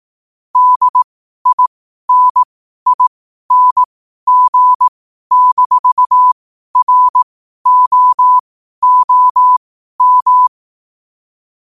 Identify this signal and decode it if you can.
Morse code, heard as a keyed tone, about 18 wpm, spaced as for 12 wpm, 1000 Hz. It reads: DINING-ROOM
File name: qotdfiles / 12wpm